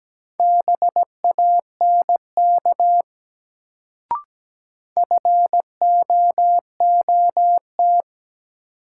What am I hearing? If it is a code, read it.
Morse code, heard as a keyed tone, about 17 wpm, 700 Hz. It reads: BANK FOOT
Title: Sets of 5 Words - New Word Drill - 17wpm